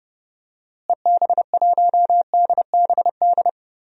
Morse code, heard as a keyed tone, about 30 wpm, 700 Hz. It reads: EB1DBD
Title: DXCC-100 Call Signs - 6 Characters - Repeated 3x - 30wpm